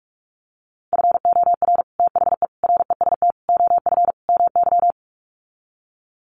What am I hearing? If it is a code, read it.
Morse code, heard as a keyed tone, about 44 wpm, 700 Hz. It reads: FOR THE REST OF MY
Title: Sets of 5 Words 45wpm